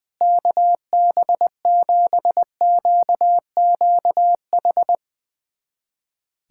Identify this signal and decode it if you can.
Morse code, heard as a keyed tone, about 20 wpm, 700 Hz. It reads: KB7QQH
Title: USA Callsign DE Callsign 20wpm